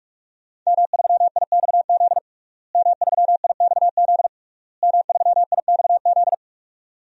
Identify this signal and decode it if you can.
Morse code, heard as a keyed tone, about 42 wpm, 700 Hz. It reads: M3IXZ M3IXZ M3IXZ
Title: DXCC-100 Call Signs - 5 Characters - Repeated 3x - 45wpm